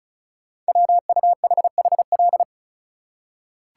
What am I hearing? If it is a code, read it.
Morse code, heard as a keyed tone, about 36 wpm, 690 Hz. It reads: WUHHL
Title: Code Groups - 5 Letters - 35wpm